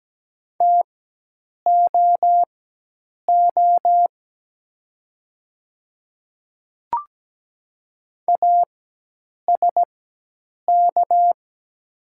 Morse code, heard as a keyed tone, about 17 wpm, spaced as for 8 wpm, 700 Hz. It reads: TOO ASK
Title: 4x Character Spacing - Top 200 Words - 17wpm